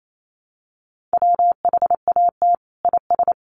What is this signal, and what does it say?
Morse code, heard as a keyed tone, about 28 wpm, 690 Hz. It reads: WHAT IS
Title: Sentences from Top 900 Words 28wpm